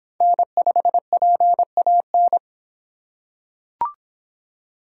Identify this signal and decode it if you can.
Morse code, heard as a keyed tone, about 24 wpm, 700 Hz. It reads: N5PAN